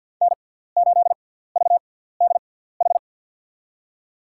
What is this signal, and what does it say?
Morse code, heard as a keyed tone, about 48 wpm, spaced as for 17 wpm, 700 Hz. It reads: N8VDH
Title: USA Call Signs - 6x Character Spacing - 50wpm